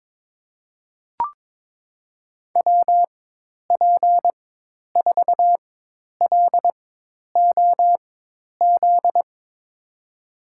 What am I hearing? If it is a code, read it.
Morse code, heard as a keyed tone, about 22 wpm, spaced as for 10 wpm, 700 Hz. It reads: WP4LOZ
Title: USA Call Signs - 6 Characters - 4x Character Spacing - 22wpm